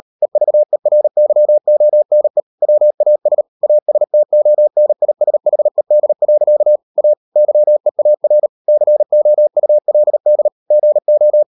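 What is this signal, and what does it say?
Morse code, heard as a keyed tone, about 38 wpm, 600 Hz. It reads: EVERYONE WAS ASTONISHED. A YEAR COULD GO
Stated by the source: Morse code practice at 35–39 wpm sending English text of Ken Follett's novel Never, from CW med Gnister.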